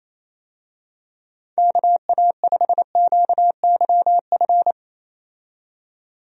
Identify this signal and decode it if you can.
Morse code, heard as a keyed tone, about 28 wpm, 700 Hz. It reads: KA5QYF
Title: USA Call Signs - 6 Characters - 28wpm